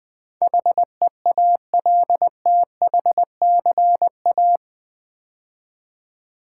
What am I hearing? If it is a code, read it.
Morse code, heard as a keyed tone, about 20 wpm, 700 Hz. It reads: HEALTHCA
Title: Build A Word - 4 Letters At A Time - 10-Letter Words - 20wpm